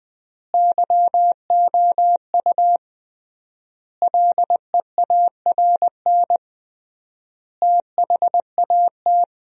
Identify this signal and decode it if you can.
Morse code, heard as a keyed tone, about 21 wpm, 700 Hz. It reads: YOU LEARN THAT
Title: 3x Word-Spacing - Sentences from Top 300 Words - 20wpm